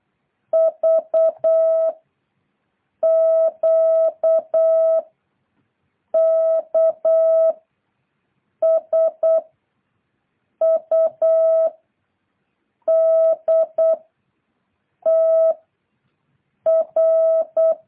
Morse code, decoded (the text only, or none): V Q K S U D T R